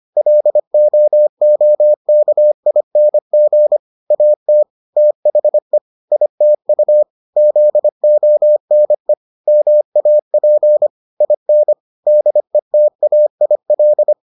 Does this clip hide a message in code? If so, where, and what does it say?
Yes: Morse code from 0.2 to 14.1 s, heard as a keyed tone, about 25 wpm, 600 Hz. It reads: LOOKING AT THE ITU ZONE MAP IN DETAIL